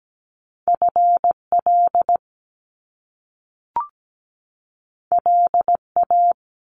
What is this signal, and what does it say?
Morse code, heard as a keyed tone, about 17 wpm, 700 Hz. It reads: FL LA
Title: US States 17wpm